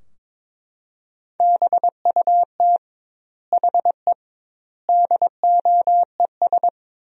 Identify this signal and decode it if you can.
Morse code, heard as a keyed tone, about 22 wpm, 700 Hz. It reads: BUT HE DOES
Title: 2x Word-Spacing - Sentences from Top 200 Words - 22wpm